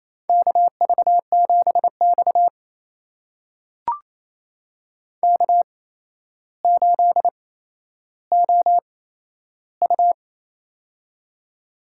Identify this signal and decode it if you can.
Morse code, heard as a keyed tone, about 28 wpm, spaced as for 12 wpm, 700 Hz. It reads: KV7X K8OU
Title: USA Call Signs - 8x Character Spacing - 28wpm